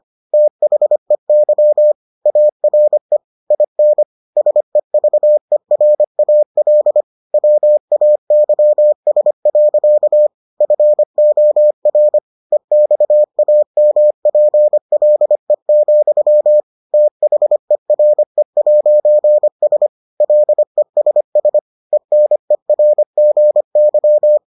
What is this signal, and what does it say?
Morse code, heard as a keyed tone, about 25 wpm, 600 Hz. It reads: THEY ARE IN SEVERAL WAYS. FOR EXAMPLE, THERE'S LESS ENERGY